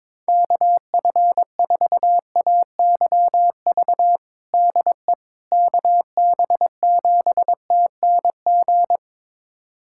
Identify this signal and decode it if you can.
Morse code, heard as a keyed tone, about 22 wpm, 700 Hz. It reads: KF4AYV DE KB7TNG
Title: USA Callsign DE Callsign - 6 Characters - 22wpm